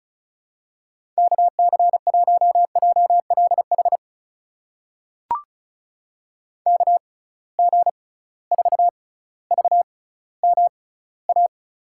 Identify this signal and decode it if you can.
Morse code, heard as a keyed tone, about 36 wpm, spaced as for 18 wpm, 700 Hz. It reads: KC1JLH KG4VMA
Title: USA Call Signs - 6 Characters - 6x Character Spacing - 35wpm